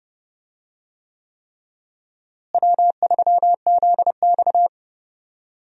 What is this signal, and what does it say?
Morse code, heard as a keyed tone, about 32 wpm, 700 Hz. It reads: W3ZX
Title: USA Call Signs - 4 Characters - Repeated 3x - 30wpm